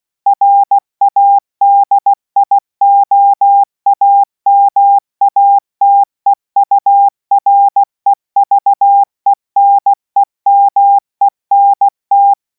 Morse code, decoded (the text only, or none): RADIOAMATEUREVENEMENT